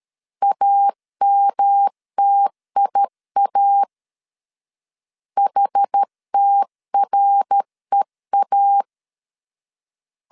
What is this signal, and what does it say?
Morse code, heard as a keyed tone, about 13 wpm, 800 Hz. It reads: AMTIA HTREA
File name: CW / DET / 1 3 1500C S A 12wpm 2905